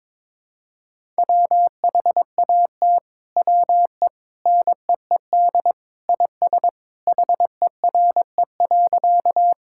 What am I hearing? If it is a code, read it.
Morse code, heard as a keyed tone, about 22 wpm, 700 Hz. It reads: WHAT WE NEED IS HERE.
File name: practice / 22wpm / Sentences with 2-Word Phrases - 22wpm